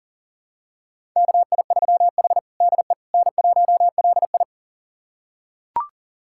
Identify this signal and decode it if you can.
Morse code, heard as a keyed tone, about 37 wpm, 700 Hz. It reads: KI3H DE N1LI
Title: USA Callsign DE Callsign - 4 Characters - 40wpm